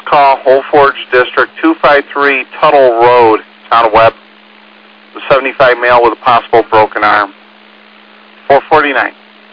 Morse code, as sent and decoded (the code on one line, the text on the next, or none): none